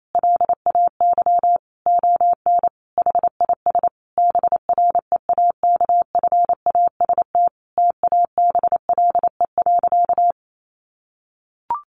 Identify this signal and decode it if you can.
Morse code, heard as a keyed tone, about 28 wpm, 700 Hz. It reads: LAY ON HIS BREAKFAST TABLE.
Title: Sentences from Top 1500 Words 28wpm